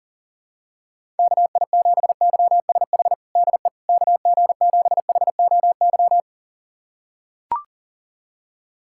Morse code, heard as a keyed tone, about 40 wpm, 700 Hz. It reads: KI7YSH DE KG7HOY
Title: USA Callsign DE Callsign 40wpm